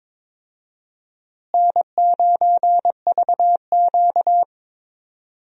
Morse code, heard as a keyed tone, about 22 wpm, 700 Hz. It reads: N9VQ
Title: USA Call Signs - 4 Characters - Repeated 2x - 22wpm